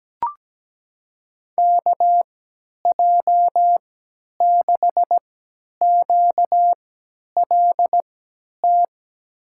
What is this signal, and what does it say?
Morse code, heard as a keyed tone, about 17 wpm, spaced as for 10 wpm, 700 Hz. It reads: KJ6QLT
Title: USA Call Signs - 6 Characters - 3x Character Spacing - 17wpm